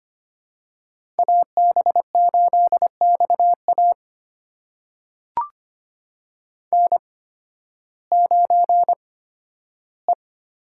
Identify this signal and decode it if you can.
Morse code, heard as a keyed tone, about 25 wpm, spaced as for 13 wpm, 700 Hz. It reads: AB8XA N9E
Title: USA Call Signs - 8x Character Spacing - 25wpm